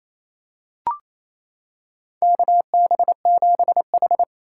Morse code, heard as a keyed tone, about 28 wpm, 700 Hz. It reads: KB7H
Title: K1USN SST QSO Practice - 28wpm